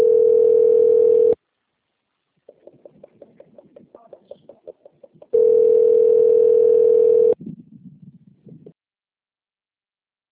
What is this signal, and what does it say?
Ringback tone (North America), 440 + 480 Hz.